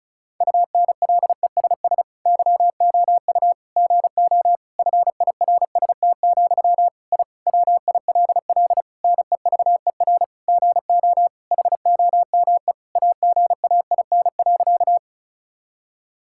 Morse code, heard as a keyed tone, about 34 wpm, 700 Hz. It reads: UNLESS YOU GO FIRST, I WILL NEVER GO HOME AGAIN.
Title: Sentences from Top 1000 Words - Review Part 6 - 35wpm